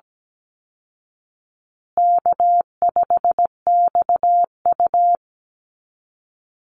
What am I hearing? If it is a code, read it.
Morse code, heard as a keyed tone, about 17 wpm, 700 Hz. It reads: K5XU
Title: Easy Call Signs Repeated 2x - 17wpm